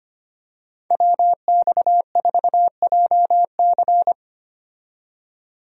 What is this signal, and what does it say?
Morse code, heard as a keyed tone, about 25 wpm, 700 Hz. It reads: WX4JC
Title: USA Call Signs - 5 Characters - 25wpm